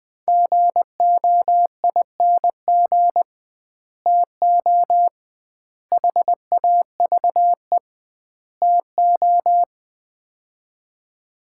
Morse code, heard as a keyed tone, about 21 wpm, 700 Hz. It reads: GOING TO HAVE TO
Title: 2x Word-Spacing - Sets of 4 Words - 20wpm